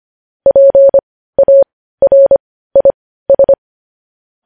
Morse code, heard as a keyed tone, about 25 wpm, spaced as for 15 wpm, 550 Hz. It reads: PARIS